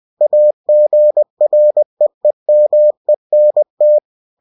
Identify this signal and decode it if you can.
Morse code, heard as a keyed tone, about 20 wpm, 600 Hz. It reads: AGREEMENT